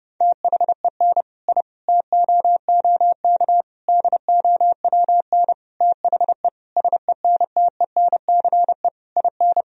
Morse code, heard as a keyed tone, about 32 wpm, 700 Hz. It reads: THEN I TOOK DOWN THE SENTENCE IN
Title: Sentences from Top 1000 Words - Review Part 1 - 30wpm